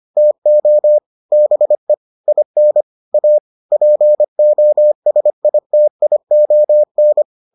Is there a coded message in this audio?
Yes: Morse code from 0.2 to 7.2 s, heard as a keyed tone, about 25 wpm, 600 Hz. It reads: TO BE IN A POSITION